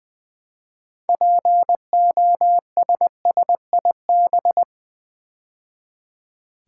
Morse code, heard as a keyed tone, about 20 wpm, 700 Hz. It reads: POSSIB
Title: Build A Word - 3 Letters At A Time - 8-Letter Words - 20wpm